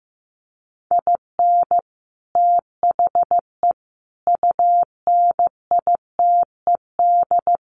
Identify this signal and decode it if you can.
Morse code, heard as a keyed tone, about 15 wpm, 700 Hz. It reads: IN THE UNITED
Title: Sets of 3 Words 15wpm